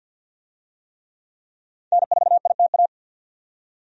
Morse code, heard as a keyed tone, about 51 wpm, 700 Hz. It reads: N4ITA
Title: USA Call Signs - 5 Characters - 50wpm